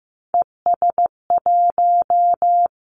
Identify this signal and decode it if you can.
Morse code, heard as a keyed tone, about 15 wpm, 700 Hz. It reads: ES1
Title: Callsign Prefixes for Top 100 DXCC - 15wpm